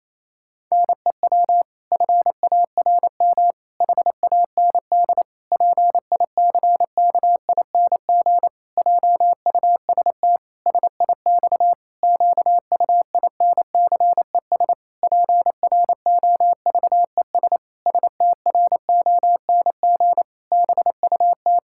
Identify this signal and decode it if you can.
Morse code, heard as a keyed tone, about 27 wpm, 700 Hz. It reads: NEW FARM HAND PICKING JUST SIX QUINCES PROVES STRONG BUT
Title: Pangram Challenge v2 - 28wpm